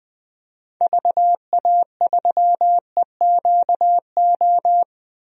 Morse code, heard as a keyed tone, about 20 wpm, 700 Hz. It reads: VA3EQO
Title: DXCC-100 Call Signs - 6 Characters - 20wpm